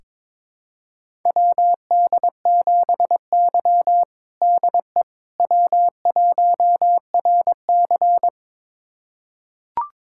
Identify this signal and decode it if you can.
Morse code, heard as a keyed tone, about 22 wpm, 700 Hz. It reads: WD7Y DE W1RC
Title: USA Callsign DE Callsign - 4 Characters - 22wpm